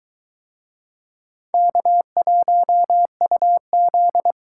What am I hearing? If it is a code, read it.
Morse code, heard as a keyed tone, about 22 wpm, 700 Hz. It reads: K1UZ